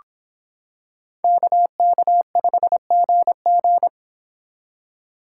Morse code, heard as a keyed tone, about 26 wpm, 700 Hz. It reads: KK5GG